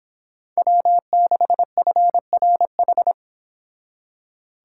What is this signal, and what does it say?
Morse code, heard as a keyed tone, about 28 wpm, 700 Hz. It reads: W6FRH